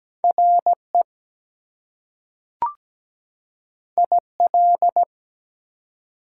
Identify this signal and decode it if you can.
Morse code, heard as a keyed tone, about 17 wpm, 700 Hz. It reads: RE IL
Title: Prefixes and Suffixes 17wpm